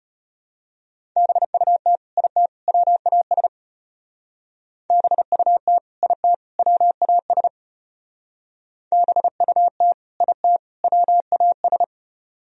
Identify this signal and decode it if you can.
Morse code, heard as a keyed tone, about 30 wpm, 700 Hz. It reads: BUT IT WAS BUT IT WAS BUT IT WAS
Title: Speed-Racing - Sets of 3 Words 25wpm